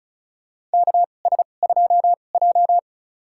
Morse code, heard as a keyed tone, about 36 wpm, spaced as for 25 wpm, 700 Hz. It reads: KS2J
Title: USA Call Signs - 4 Characters - 2x Character Spacing - 35wpm